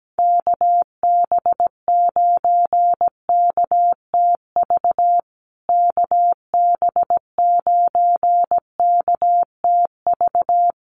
Morse code, heard as a keyed tone, about 17 wpm, 700 Hz. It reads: KB9KTV KB9KTV
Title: USA Call Signs Repeated 2x 17wpm